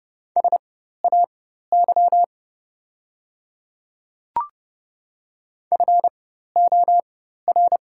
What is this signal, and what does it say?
Morse code, heard as a keyed tone, about 30 wpm, spaced as for 14 wpm, 700 Hz. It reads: SAY FOR
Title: 4x Character Spacing - Top 300 Words - 30wpm